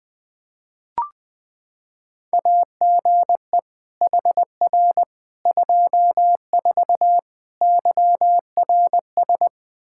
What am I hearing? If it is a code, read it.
Morse code, heard as a keyed tone, about 21 wpm, 700 Hz. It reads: AGE HR 24 YRS
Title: K1USN SST QSO Practice 20wpm